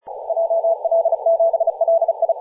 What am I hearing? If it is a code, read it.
Morse code, heard as a keyed tone, about 37 wpm, 660 Hz, with radio noise. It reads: JL7LI
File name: posts / 2016-nov-11-2050z-tl8ao-10-112mhz-cw